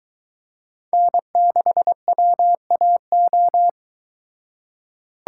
Morse code, heard as a keyed tone, about 22 wpm, 700 Hz. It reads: N6WAO